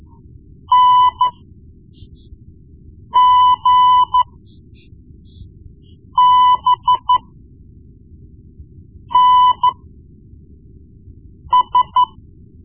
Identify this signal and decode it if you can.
Morse code, heard as a keyed tone, about 13 wpm, spaced as for 4 wpm, 1000 Hz. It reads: NGBNS
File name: Morse Lessons / Morse Lessons / Tape-1B